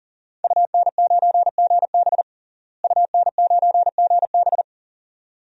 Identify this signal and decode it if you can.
Morse code, heard as a keyed tone, about 37 wpm, 700 Hz. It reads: UN9GB UN9GB
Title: DXCC-100 Call Signs - 5 Characters - Repeated 2x - 40wpm